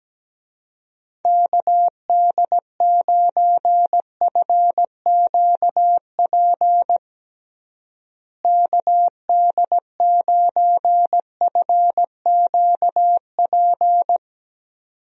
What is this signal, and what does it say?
Morse code, heard as a keyed tone, about 17 wpm, 700 Hz. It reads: KD9FQP KD9FQP
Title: USA Call Signs - 6 Characters - Repeated 2x - 17wpm